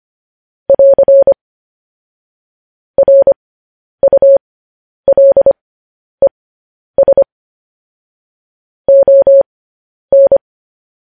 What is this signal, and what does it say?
Morse code, heard as a keyed tone, about 25 wpm, spaced as for 10 wpm, 550 Hz. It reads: <AR> RULES ON
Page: news headlines in morse code at 10 wpm